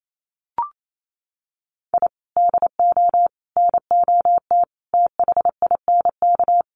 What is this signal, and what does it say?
Morse code, heard as a keyed tone, about 28 wpm, 700 Hz. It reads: I DO NOT THINK
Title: Sets of 5 Words Encore2 28wpm